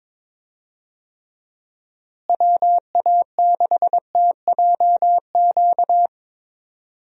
Morse code, heard as a keyed tone, about 22 wpm, 700 Hz. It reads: WA6TJQ
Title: USA Call Signs - 6 Characters - Repeated 3x - 22wpm